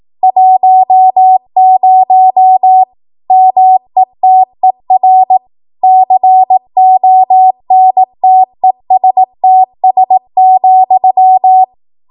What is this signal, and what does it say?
Morse code, heard as a keyed tone, about 18 wpm, 750 Hz. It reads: 10 METER CONTESTS,
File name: Archive / 18 WPM / 18WPM